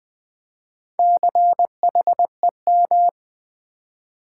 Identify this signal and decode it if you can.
Morse code, heard as a keyed tone, about 20 wpm, 700 Hz. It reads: CHEM
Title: Build A Word - 2 Letters At A Time - 8-Letter Words - 20wpm